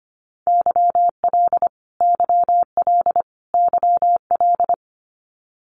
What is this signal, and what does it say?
Morse code, heard as a keyed tone, about 25 wpm, 700 Hz. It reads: YL YL YL
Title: Abbreviations Repeated 3x - 25wpm